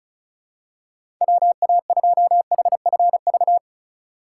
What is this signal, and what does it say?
Morse code, heard as a keyed tone, about 36 wpm, 700 Hz. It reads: WA2HFV
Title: USA Call Signs - 6 Characters - 4x Character Spacing - 35wpm